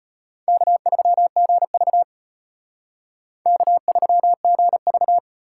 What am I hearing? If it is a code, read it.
Morse code, heard as a keyed tone, about 32 wpm, 700 Hz. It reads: K3GV K3GV